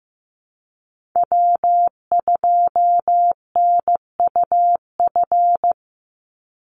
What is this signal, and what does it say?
Morse code, heard as a keyed tone, about 15 wpm, 700 Hz. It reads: W2NUF
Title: USA Call Signs 15wpm